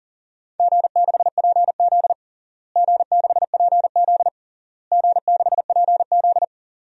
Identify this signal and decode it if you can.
Morse code, heard as a keyed tone, about 40 wpm, 700 Hz. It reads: G6PZ G6PZ G6PZ
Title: DXCC-100 Call Signs - 4 Characters - Repeated 3x - 40wpm